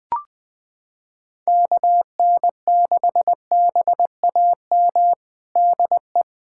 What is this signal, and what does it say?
Morse code, heard as a keyed tone, about 20 wpm, 700 Hz. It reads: KN6BAM DE
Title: USA Callsign DE Callsign - 6 Characters - 20wpm